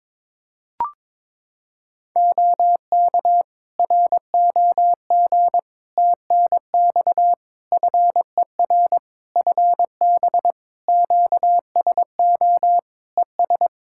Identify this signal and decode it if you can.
Morse code, heard as a keyed tone, about 22 wpm, 700 Hz. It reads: OK ROG TNX FER FB QSO ES
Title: Practice QSO - Exchange 4 - Ending - 22wpm